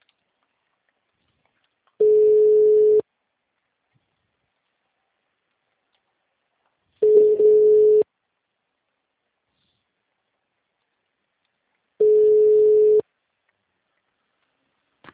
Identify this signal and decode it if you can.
Ringback tone (Europe, 425 Hz).